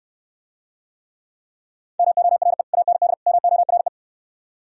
Morse code, heard as a keyed tone, about 69 wpm, 700 Hz. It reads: MORE AND MORE